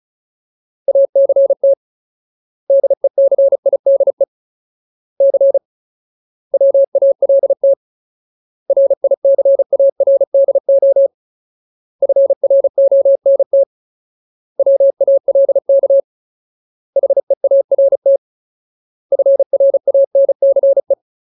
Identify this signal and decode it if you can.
Morse code, heard as a keyed tone, about 34 wpm, 550 Hz. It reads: ACT DECIDE C WALT RICARDO FRONT WALK HEART FRANCE
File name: Ultimate Word List / 35WPM Ultimate